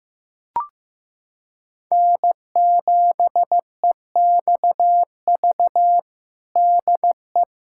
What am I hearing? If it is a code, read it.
Morse code, heard as a keyed tone, about 15 wpm, 700 Hz. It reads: N7EXV DE
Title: USA Callsign DE Callsign - 5 Characters - 15wpm